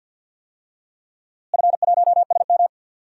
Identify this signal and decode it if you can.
Morse code, heard as a keyed tone, about 52 wpm, 700 Hz. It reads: F1SM
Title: DXCC-100 Call Signs - 4 Characters - Repeated 3x - 50wpm